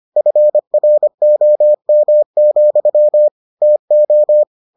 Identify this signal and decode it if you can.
Morse code, heard as a keyed tone, about 25 wpm, 600 Hz. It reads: FROM, TO